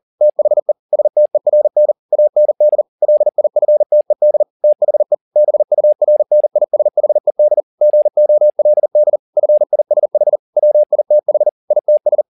Morse code, heard as a keyed tone, about 40 wpm, 600 Hz. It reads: THE STERN AND LIFTED THE BURNISHED GOLD FISH WITH ITS
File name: the old man and the sea 2 40wpm